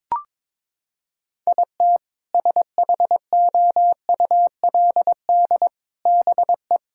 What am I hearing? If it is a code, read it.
Morse code, heard as a keyed tone, about 21 wpm, 700 Hz. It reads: IT SHOULD BE
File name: practice / 22wpm / Sentences with 2-Word Phrases - 22wpm